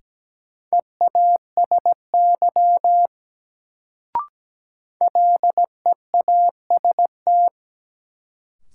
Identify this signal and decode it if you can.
Morse code, heard as a keyed tone, about 17 wpm, 700 Hz. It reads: EASY LEAST
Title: Sentences from Top 700 Words - New Word Drill - 17wpm